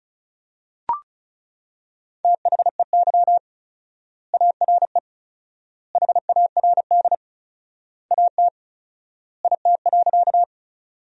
Morse code, heard as a keyed tone, about 34 wpm, 700 Hz. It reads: THEY ARE HARD AT IT.
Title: 4x Word-Spacing - Sentences from Top 300 Words - 35wpm